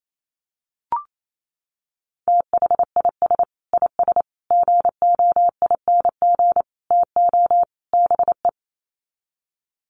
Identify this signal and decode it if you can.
Morse code, heard as a keyed tone, about 28 wpm, 700 Hz. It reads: THIS IS GOING TO BE
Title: Sets of 5 Words 28wpm